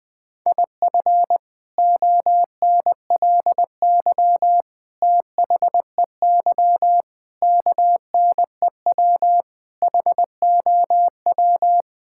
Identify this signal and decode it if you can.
Morse code, heard as a keyed tone, about 21 wpm, 700 Hz. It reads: IF ONLY THEY KNEW HOW
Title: Sentences from Top 1000 Words - Review Part 6 - 20wpm